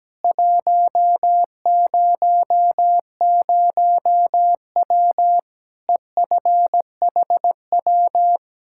Morse code, heard as a keyed tone, about 17 wpm, 700 Hz. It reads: 100W EFHW
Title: Practice QSO - Exchange 2 - Info - 17wpm